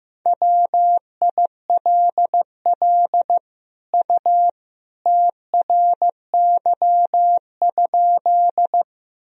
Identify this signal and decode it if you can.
Morse code, heard as a keyed tone, about 15 wpm, 700 Hz. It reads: WILL U TRY?
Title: Sentences from Top 200 Words and Abbreviations 15wpm